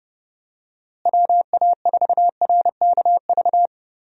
Morse code, heard as a keyed tone, about 32 wpm, 700 Hz. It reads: WA4RKV
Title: USA Call Signs - 6 Characters - Repeated 3x - 30wpm